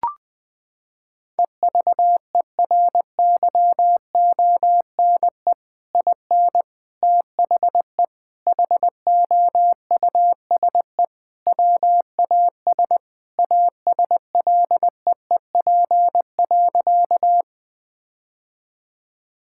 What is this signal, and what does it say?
Morse code, heard as a keyed tone, about 20 wpm, 700 Hz. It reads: EVERYONE IN THE HOUSE WAS ASLEEP.
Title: Sentences with 3-Word Phrases - 20wpm